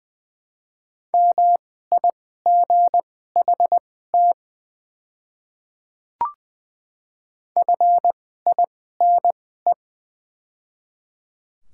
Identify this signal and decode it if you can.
Morse code, heard as a keyed tone, about 20 wpm, spaced as for 14 wpm, 700 Hz. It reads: MIGHT FINE
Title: 2x Character Spacing - Top 500 Words - 20wpm